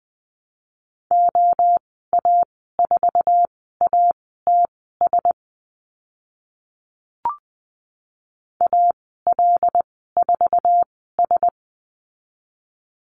Morse code, heard as a keyed tone, about 20 wpm, spaced as for 14 wpm, 700 Hz. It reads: OA4ATS AL4S
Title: 082 Lesson 14g - Number 2 - Call Signs 20wpm